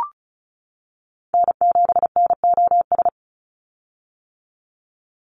Morse code, heard as a keyed tone, about 34 wpm, 700 Hz. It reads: N7NOS
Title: USA Call Signs 35wpm